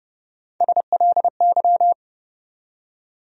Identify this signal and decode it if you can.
Morse code, heard as a keyed tone, about 30 wpm, 700 Hz. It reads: SLY
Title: USA Call Sign Postfixes - 3 Letters 30wpm